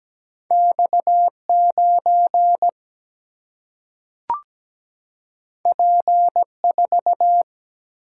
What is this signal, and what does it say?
Morse code, heard as a keyed tone, about 17 wpm, 700 Hz. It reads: X9 P4
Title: Single-Letter Single-Number 17wpm